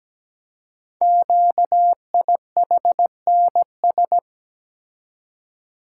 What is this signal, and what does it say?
Morse code, heard as a keyed tone, about 17 wpm, 700 Hz. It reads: QIHNS
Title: Code Groups - 5 Letters - 17wpm